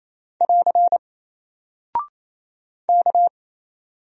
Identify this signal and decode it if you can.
Morse code, heard as a keyed tone, about 28 wpm, 700 Hz. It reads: <AR> K
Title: Single Character (Letter:Number:Punctuation:Prosign) Alt 28wpm